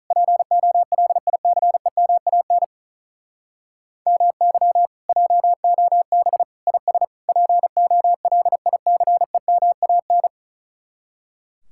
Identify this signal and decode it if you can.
Morse code, heard as a keyed tone, about 44 wpm, 700 Hz. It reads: POLICEMAN MY JOB IS POLICEMAN